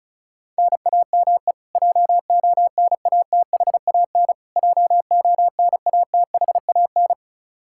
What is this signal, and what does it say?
Morse code, heard as a keyed tone, about 34 wpm, 700 Hz. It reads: NAME JONATHAN JONATHAN
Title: Practice QSO - Exchange 1 - Beginning - 35wpm